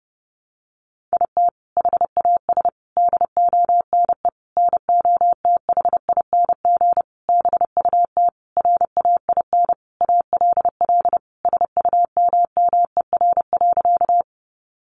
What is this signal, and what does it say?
Morse code, heard as a keyed tone, about 30 wpm, 700 Hz. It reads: IT HAS DONE NOTHING BUT RAIN ALL SUMMER.
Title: Sentences from Top 1100 Words 30wpm